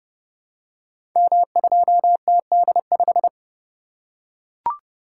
Morse code, heard as a keyed tone, about 29 wpm, 700 Hz. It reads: M2TD5
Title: Code Groups - 5 Letter:Numbers - 30wpm